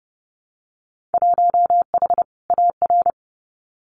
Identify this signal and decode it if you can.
Morse code, heard as a keyed tone, about 30 wpm, 700 Hz. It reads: 1H AR
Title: Winter Field Day Exchanges 30wpm